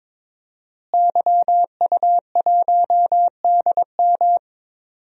Morse code, heard as a keyed tone, about 22 wpm, 700 Hz. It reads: YU1DM
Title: DXCC-100 Call Signs - 5 Characters - Repeated 3x - 22wpm